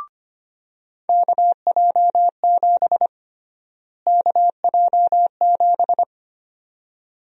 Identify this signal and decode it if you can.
Morse code, heard as a keyed tone, about 25 wpm, 700 Hz. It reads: KJ7 KJ7
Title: USA Call Sign Prefixes - Repeated 2x - 25wpm